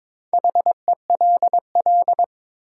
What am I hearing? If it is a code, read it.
Morse code, heard as a keyed tone, about 22 wpm, 700 Hz. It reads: HELL
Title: Sets of 5 Words Encore - New Word Drill - 22wpm